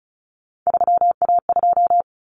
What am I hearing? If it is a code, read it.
Morse code, heard as a keyed tone, about 34 wpm, 700 Hz. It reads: 3A2
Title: Common Cut Numbers - 3-Digit 35wpm